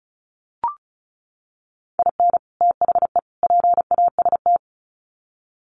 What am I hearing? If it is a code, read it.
Morse code, heard as a keyed tone, about 34 wpm, 700 Hz. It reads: IN THE PAST
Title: Sets of 3 Words 35wpm